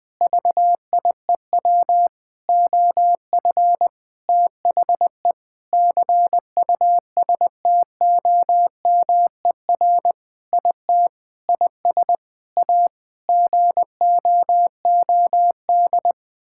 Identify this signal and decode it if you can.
Morse code, heard as a keyed tone, about 20 wpm, 700 Hz. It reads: VIEW OF THE CUSTOMER IT IS A GOOD
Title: Sentences with 7-Word Phrases - 20wpm